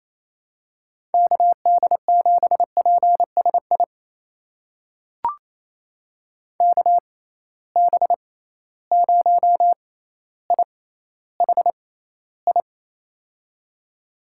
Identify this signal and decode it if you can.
Morse code, heard as a keyed tone, about 28 wpm, spaced as for 14 wpm, 700 Hz. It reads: KD7PSI KB0IHI
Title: USA Call Signs - 6 Characters - 6x Character Spacing - 28wpm